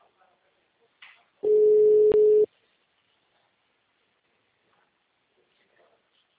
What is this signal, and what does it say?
Ringback tone (Europe, 425 Hz), with speech.